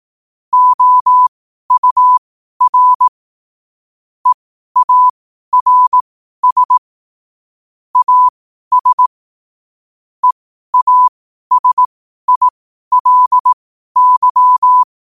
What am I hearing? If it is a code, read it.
Morse code, heard as a keyed tone, about 18 wpm, spaced as for 12 wpm, 1000 Hz. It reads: OUR EARS AS EASILY
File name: qotdfiles / 12wpm